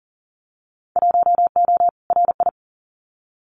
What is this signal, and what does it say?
Morse code, heard as a keyed tone, about 42 wpm, 700 Hz. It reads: 1O RI
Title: Winter Field Day Exchanges 40wpm